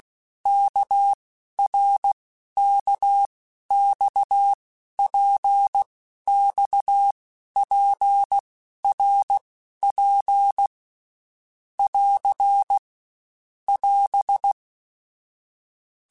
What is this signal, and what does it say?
Morse code, heard as a keyed tone, about 16 wpm, 800 Hz. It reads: K R K X P X P R P <AR> <AS>
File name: Basic Letter Groups By7Groups / RKPX / RKPX@15